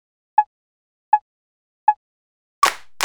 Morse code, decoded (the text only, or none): none